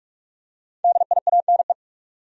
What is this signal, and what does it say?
Morse code, heard as a keyed tone, about 46 wpm, 700 Hz. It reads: DIANE
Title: USA Common Names - Max 5 Letters - 45wpm